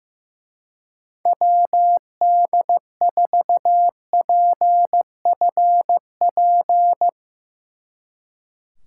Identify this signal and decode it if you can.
Morse code, heard as a keyed tone, about 16 wpm, 700 Hz. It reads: WD4PFP
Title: Speed Racing - USA 2x3 Call Signs 15wpm